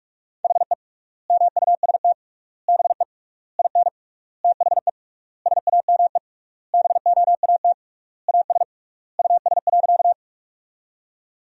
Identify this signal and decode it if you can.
Morse code, heard as a keyed tone, about 46 wpm, 700 Hz. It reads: HE MUST BE IN THE SAME BOAT AS US.
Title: 3x Word-Spacing - Sentences from Top 500 Words - 45wpm